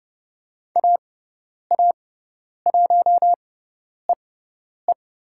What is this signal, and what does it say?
Morse code, heard as a keyed tone, about 30 wpm, spaced as for 10 wpm, 700 Hz. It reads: AA1EE
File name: Lesson 18g - Letter m - Call Signs 30wpm